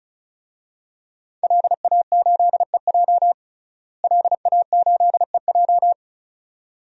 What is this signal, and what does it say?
Morse code, heard as a keyed tone, about 36 wpm, 700 Hz. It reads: LA8EJ LA8EJ
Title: DXCC-100 Call Signs - 5 Characters - Repeated 2x - 35wpm